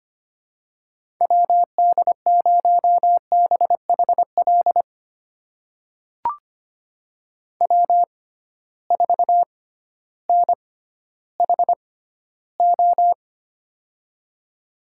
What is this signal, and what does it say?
Morse code, heard as a keyed tone, about 25 wpm, spaced as for 13 wpm, 700 Hz. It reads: WD0BHL W4NHO
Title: USA Call Signs - 6x Character Spacing - 25wpm